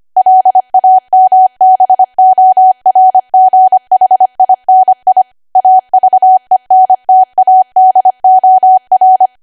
Morse code, heard as a keyed tone, about 25 wpm, 750 Hz. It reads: LAMBORGHINI AVENTADOR
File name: cwa / W1AW-Level3 / 25WPM